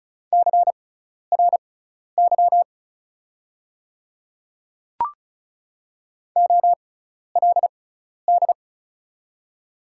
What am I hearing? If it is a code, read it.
Morse code, heard as a keyed tone, about 34 wpm, spaced as for 12 wpm, 700 Hz. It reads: CRY OLD
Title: 6x Character Spacing - Top 500 Words - 35wpm